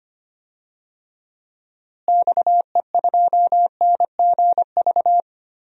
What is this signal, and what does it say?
Morse code, heard as a keyed tone, about 25 wpm, 700 Hz. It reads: XE2NGV
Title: DXCC-100 Call Signs - 6 Characters - Repeated 3x - 25wpm